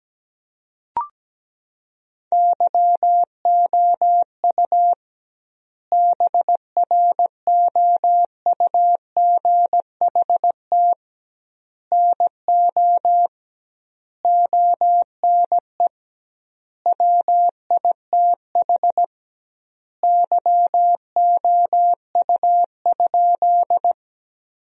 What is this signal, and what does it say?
Morse code, heard as a keyed tone, about 17 wpm, 700 Hz. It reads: YOU BROUGHT NO ONE WITH YOU?
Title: 2x Word-Spacing - Sentences from Top 500 Words - 17wpm